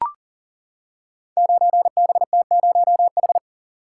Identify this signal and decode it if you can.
Morse code, heard as a keyed tone, about 37 wpm, 700 Hz. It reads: 9BT0H
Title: Code Groups - 5 Letter:Numbers - 40wpm